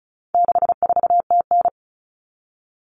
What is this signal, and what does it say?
Morse code, heard as a keyed tone, about 34 wpm, 700 Hz. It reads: 64TN
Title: Common Cut Numbers - 4-Digit 35wpm